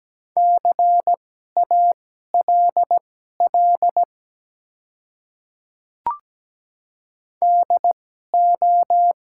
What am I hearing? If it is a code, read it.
Morse code, heard as a keyed tone, about 17 wpm, spaced as for 12 wpm, 700 Hz. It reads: CALL DO
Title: 2x Character Spacing - Top 100 Words - 17wpm